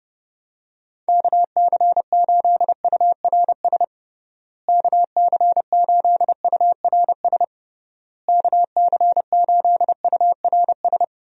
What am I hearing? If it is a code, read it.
Morse code, heard as a keyed tone, about 30 wpm, 700 Hz. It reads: KC8URS KC8URS KC8URS
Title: USA Call Signs - 6 Characters - Repeated 3x - 30wpm